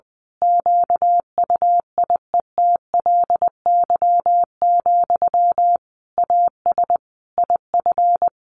Morse code, heard as a keyed tone, about 20 wpm, 700 Hz. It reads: QUIETLY, AS IF
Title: A Sound Of Thunder 20wpm